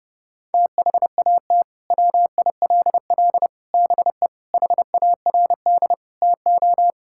Morse code, heard as a keyed tone, about 30 wpm, 700 Hz. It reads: THAT WILL BE HARD TO
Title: Sentences from Top 1000 Words - Review Part 5 - 30wpm